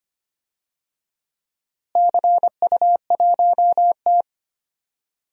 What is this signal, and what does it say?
Morse code, heard as a keyed tone, about 25 wpm, 700 Hz. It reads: CU1T
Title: DXCC-100 Call Signs - 4 Characters - Repeated 3x - 25wpm